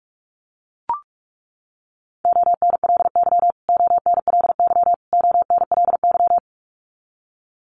Morse code, heard as a keyed tone, about 44 wpm, 700 Hz. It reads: ONLY ONLY ONLY
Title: Top 200 Words Repeated 45wpm